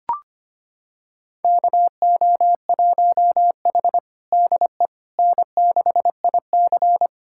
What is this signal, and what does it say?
Morse code, heard as a keyed tone, about 25 wpm, 700 Hz. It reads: KO1H DE N6IC
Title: USA Callsign DE Callsign - 4 Characters - 25wpm